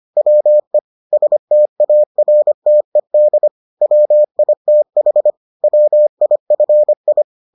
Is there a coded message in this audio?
Yes: Morse code from 0.2 to 7.2 s, heard as a keyed tone, about 25 wpm, 600 Hz. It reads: WE STARTED WITH WIFI